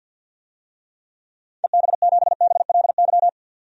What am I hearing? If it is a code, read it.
Morse code, heard as a keyed tone, about 50 wpm, 700 Hz. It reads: EB7BLY
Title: DXCC-100 Call Signs - 6 Characters - 50wpm